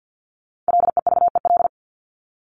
Morse code, heard as a keyed tone, about 50 wpm, 700 Hz. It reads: LEVEL
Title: Top 501-750 Words 50wpm